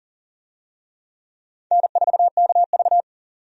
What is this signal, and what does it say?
Morse code, heard as a keyed tone, about 40 wpm, 700 Hz. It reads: N4KV
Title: USA Call Signs - 4 Characters - 40wpm